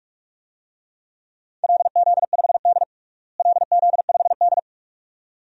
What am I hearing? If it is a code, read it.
Morse code, heard as a keyed tone, about 46 wpm, 700 Hz. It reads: LZ5D LZ5D
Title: DXCC-100 Call Signs - 4 Characters - Repeated 2x - 45wpm